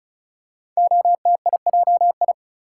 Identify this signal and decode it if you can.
Morse code, heard as a keyed tone, about 36 wpm, 700 Hz. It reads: OTIJI
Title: Code Groups - 5 Letters - 35wpm